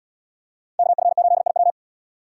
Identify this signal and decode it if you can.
Morse code, heard as a keyed tone, about 71 wpm, 700 Hz. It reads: KF8IW